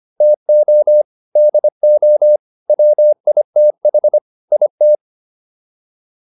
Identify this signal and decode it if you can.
Morse code, heard as a keyed tone, about 25 wpm, 600 Hz. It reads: TO DO WITH IT